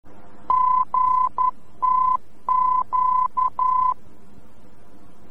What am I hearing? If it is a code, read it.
Morse code, heard as a keyed tone, about 11 wpm, 1010 Hz. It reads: GTQ
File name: france / mx / GTQ-VOR-F